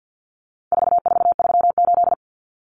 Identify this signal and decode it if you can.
Morse code, heard as a keyed tone, about 50 wpm, 700 Hz. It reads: 4438
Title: Numbers - 4-Digit 50wpm